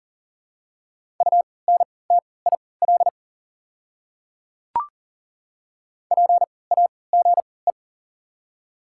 Morse code, heard as a keyed tone, about 40 wpm, spaced as for 23 wpm, 690 Hz. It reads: UNTIL PAGE
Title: 3x Character Spacing - Top 300 Words - 40wpm